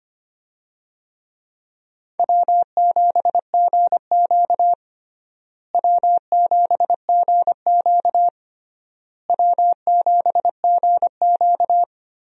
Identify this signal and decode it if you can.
Morse code, heard as a keyed tone, about 25 wpm, 700 Hz. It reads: W7GQ W7GQ W7GQ
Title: USA Call Signs - 4 Characters - Repeated 3x - 25wpm